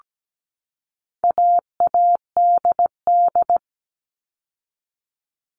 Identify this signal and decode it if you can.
Morse code, heard as a keyed tone, about 17 wpm, 700 Hz. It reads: AADD
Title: All Cut Numbers - 4-Digit 17wpm